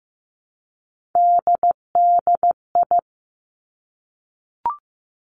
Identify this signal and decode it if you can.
Morse code, heard as a keyed tone, about 15 wpm, 700 Hz. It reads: DDI
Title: Three Challenging Letter-Number Combinations 15wpm